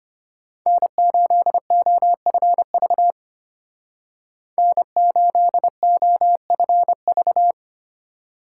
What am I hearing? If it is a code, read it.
Morse code, heard as a keyed tone, about 33 wpm, 700 Hz. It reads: N8OFV N8OFV